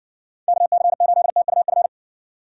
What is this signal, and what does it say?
Morse code, heard as a keyed tone, about 59 wpm, 700 Hz. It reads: KC8TUF